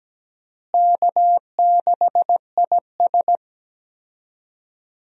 Morse code, heard as a keyed tone, about 17 wpm, 700 Hz. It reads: K6IS
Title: USA Call Signs - 4 Characters - Repeated 3x - 17wpm